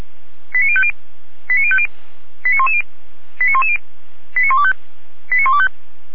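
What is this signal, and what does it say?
Five-tone selective call (ZVEI), heard as quick runs of five tones, tones of about 70 ms. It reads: 89050 89050 89110 89110 89125 89125
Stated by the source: ZVEI five-tone call received by SDR, from the feuerwehr-orlamuende.org audio server.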